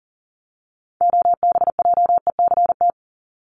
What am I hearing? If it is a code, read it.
Morse code, heard as a keyed tone, about 40 wpm, 700 Hz. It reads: OBJECT
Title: Top 500 Words 40wpm